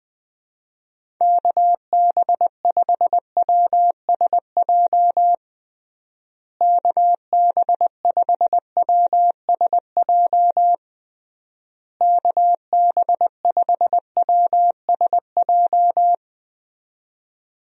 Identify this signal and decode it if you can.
Morse code, heard as a keyed tone, about 20 wpm, 700 Hz. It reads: KB5WSJ KB5WSJ KB5WSJ
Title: USA Call Signs - 6 Characters - Repeated 3x - 20wpm